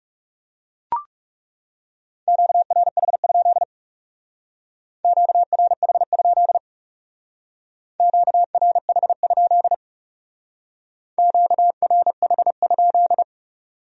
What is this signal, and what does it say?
Morse code, heard as a keyed tone, about 36 wpm, 700 Hz. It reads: QRH? QRH? QRH? QRH?
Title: Speed-Racing Q-Codes - 30wpm